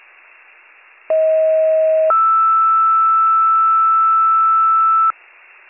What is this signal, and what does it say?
Two-tone page, heard as a one-second tone, then a three-second tone, about 640 then 1320 Hz. It is heard from 1.1 to 5.1 s.